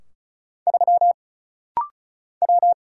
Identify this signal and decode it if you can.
Morse code, heard as a keyed tone, about 36 wpm, 700 Hz. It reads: 3 W
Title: Single Letter-Number - Rapid-Fire 35wpm